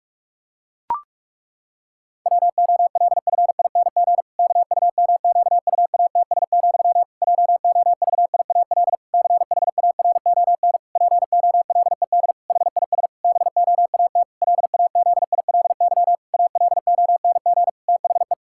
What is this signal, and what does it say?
Morse code, heard as a keyed tone, about 46 wpm, 700 Hz. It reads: WOLFING KUMQUATS, JOVIAL CHARON POLED HIS BOAT LAZILY ALONG THE
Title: Pangram Challenge v2 - 45wpm